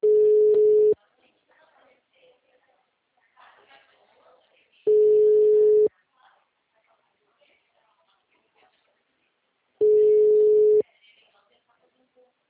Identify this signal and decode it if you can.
Ringback tone (Europe, 425 Hz), with speech.
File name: out- -2012- - - .186